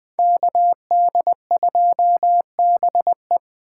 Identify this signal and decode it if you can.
Morse code, heard as a keyed tone, about 20 wpm, 700 Hz. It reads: KD2BE
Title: USA Call Signs - 6x Character Spacing - 20wpm